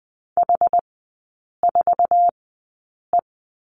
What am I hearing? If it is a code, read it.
Morse code, heard as a keyed tone, about 20 wpm, spaced as for 8 wpm, 700 Hz. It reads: H4E
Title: 060 Lesson 11e - Letter h - Three Characters 20wpm